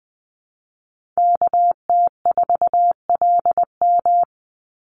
Morse code, heard as a keyed tone, about 20 wpm, 700 Hz. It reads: KT4LM
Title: USA Call Signs 20wpm